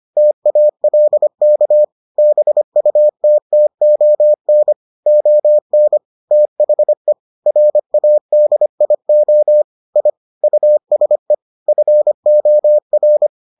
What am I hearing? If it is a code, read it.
Morse code, heard as a keyed tone, about 25 wpm, 600 Hz. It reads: TALK BUTTON ON THE RADIO I USE FOR